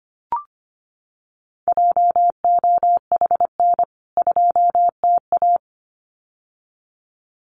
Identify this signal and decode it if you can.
Morse code, heard as a keyed tone, about 25 wpm, 700 Hz. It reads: JOHN 2TA
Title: CWT Exchange 25wpm